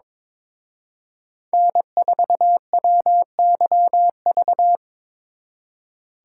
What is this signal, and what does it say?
Morse code, heard as a keyed tone, about 26 wpm, 700 Hz. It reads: N4WYV
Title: Speed Racing - USA Call Signs 22wpm